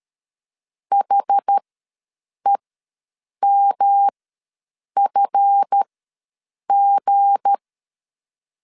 Morse code, heard as a keyed tone, about 13 wpm, spaced as for 7 wpm, 800 Hz. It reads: HEMFG
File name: LEZIONI / 8 YJ45 / 8 YJ4 ws 12wpm 3249